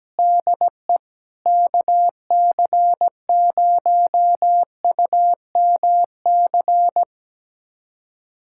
Morse code, heard as a keyed tone, about 17 wpm, 700 Hz. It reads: DE KC0UMC
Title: USA Callsign DE Callsign - 6 Characters - 17wpm